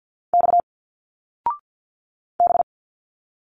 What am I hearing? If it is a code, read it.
Morse code, heard as a keyed tone, about 46 wpm, 700 Hz. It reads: X B
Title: Single Letters 50wpm